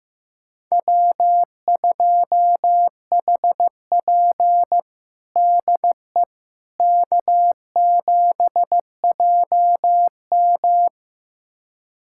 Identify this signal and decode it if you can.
Morse code, heard as a keyed tone, about 15 wpm, 700 Hz. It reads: W2HP DE K7JM
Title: USA Callsign DE Callsign - 4 Characters - 15wpm